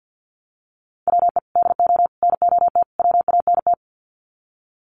Morse code, heard as a keyed tone, about 50 wpm, 700 Hz. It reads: WE DO NOT WANT
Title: Sets of 4 Words 50wpm